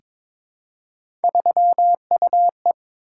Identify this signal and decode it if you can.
Morse code, heard as a keyed tone, about 22 wpm, 700 Hz. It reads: 3UE
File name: practice / 22wpm / Three Letter-Number Combinations - 22wpm